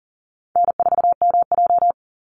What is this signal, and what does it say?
Morse code, heard as a keyed tone, about 40 wpm, 700 Hz. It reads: N4MJ
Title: NA SOTA Chaser Callsigns 40wpm